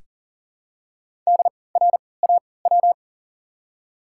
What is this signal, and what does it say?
Morse code, heard as a keyed tone, about 40 wpm, spaced as for 23 wpm, 700 Hz. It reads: DRAW
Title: 3x Character Spacing - Top 300 Words - 40wpm